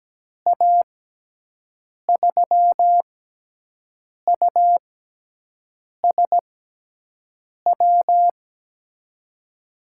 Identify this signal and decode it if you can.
Morse code, heard as a keyed tone, about 17 wpm, spaced as for 6 wpm, 700 Hz. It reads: A3USW
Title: USA Call Signs - 6 Characters - 6x Character Spacing - 17wpm